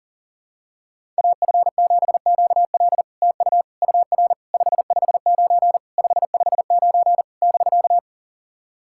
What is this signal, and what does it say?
Morse code, heard as a keyed tone, about 40 wpm, 690 Hz. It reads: AF7QL TU UR 559 559 <BK>
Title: SOTA QSO Practice 40wpm